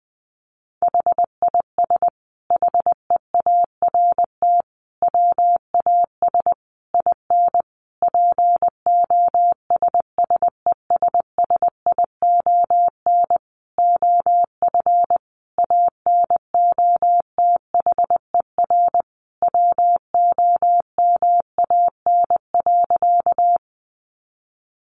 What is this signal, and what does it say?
Morse code, heard as a keyed tone, about 20 wpm, 700 Hz. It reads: HIS HEART WAS IN POSSESSION OF ANOTHER WOMAN.
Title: Sentences from Top 1400 Words 20wpm